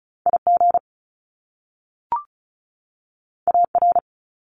Morse code, heard as a keyed tone, about 36 wpm, 700 Hz. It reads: IG AR
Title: Top Two Letter Combinations in Words 35wpm